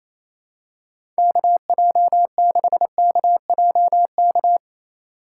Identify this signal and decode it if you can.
Morse code, heard as a keyed tone, about 27 wpm, 700 Hz. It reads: KJ6KJK
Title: USA Call Signs - 8x Character Spacing - 28wpm